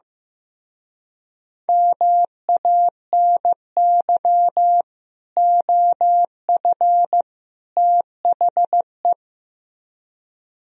Morse code, heard as a keyed tone, about 16 wpm, 700 Hz. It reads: MANY OF THE
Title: Speed-Racing - Sets of 3 Words 15wpm